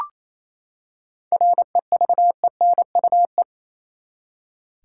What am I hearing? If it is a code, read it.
Morse code, heard as a keyed tone, about 28 wpm, 700 Hz. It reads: REVENUE
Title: Seven Letter Words - 28wpm